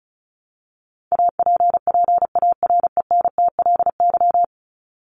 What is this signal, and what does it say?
Morse code, heard as a keyed tone, about 34 wpm, 700 Hz. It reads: APPARENTLY
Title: LY Suffix - Top 100 Words with Base Words 35wpm